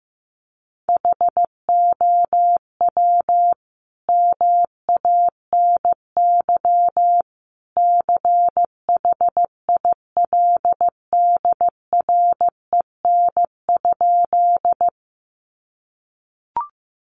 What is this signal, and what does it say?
Morse code, heard as a keyed tone, about 15 wpm, 700 Hz. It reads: HOW MANY CHILDREN?
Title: Sentences from Top 300 Words 15wpm